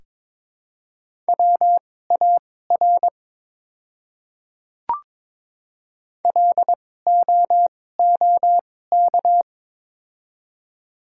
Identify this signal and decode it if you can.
Morse code, heard as a keyed tone, about 22 wpm, spaced as for 16 wpm, 700 Hz. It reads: WAR LOOK
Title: 2x Character Spacing - Top 500 Words - 22wpm